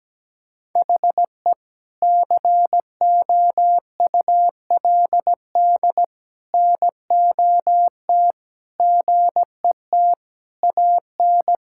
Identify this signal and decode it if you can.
Morse code, heard as a keyed tone, about 17 wpm, 700 Hz. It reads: HE COULD NOT GET AN
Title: Sentences from Top 1000 Words - Review Part 5 - 17wpm